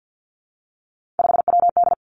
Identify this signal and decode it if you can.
Morse code, heard as a keyed tone, about 50 wpm, 700 Hz. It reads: 5WD
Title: Three Challenging Letter-Number Combinations 50wpm